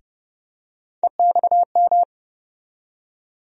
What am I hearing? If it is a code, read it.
Morse code, heard as a keyed tone, about 32 wpm, 700 Hz. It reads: EXM
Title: USA Call Sign Postfixes - 3 Letters 30wpm